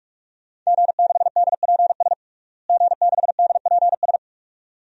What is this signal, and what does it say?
Morse code, heard as a keyed tone, about 46 wpm, 700 Hz. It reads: G6DPS G6DPS
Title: DXCC-100 Call Signs - 5 Characters - Repeated 2x - 45wpm